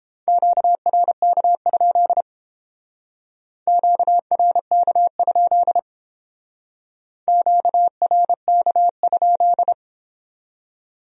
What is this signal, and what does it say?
Morse code, heard as a keyed tone, about 31 wpm, 700 Hz. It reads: QRK? QRK? QRK?